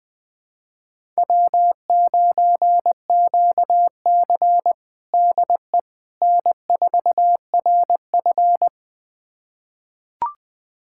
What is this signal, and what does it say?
Morse code, heard as a keyed tone, about 20 wpm, 700 Hz. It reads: W9QC DE N4RF
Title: USA Callsign DE Callsign - 4 Characters - 20wpm